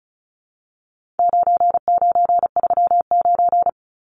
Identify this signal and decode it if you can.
Morse code, heard as a keyed tone, about 34 wpm, 700 Hz. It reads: 9939
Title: Numbers - 4-Digit 35wpm